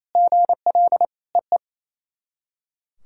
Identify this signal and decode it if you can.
Morse code, heard as a keyed tone, about 28 wpm, 690 Hz. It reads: GL EE
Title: SOTA QSO Practice 28wpm